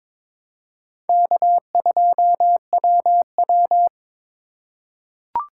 Morse code, heard as a keyed tone, about 21 wpm, 700 Hz. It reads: K2WW
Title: USA Call Signs - 4 Characters - 3x Character Spacing - 22wpm